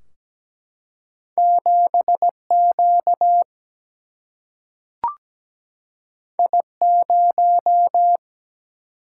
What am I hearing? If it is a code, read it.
Morse code, heard as a keyed tone, about 17 wpm, 700 Hz. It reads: 7Q I0
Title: Single-Letter Single-Number 17wpm